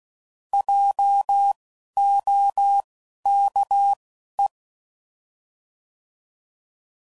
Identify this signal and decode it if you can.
Morse code, heard as a keyed tone, about 16 wpm, spaced as for 11 wpm, 800 Hz. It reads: JOKE